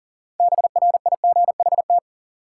Morse code, heard as a keyed tone, about 40 wpm, 700 Hz. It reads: BRIGHT
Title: Sentences from Top 1000 Words - New Word Drill - 40wpm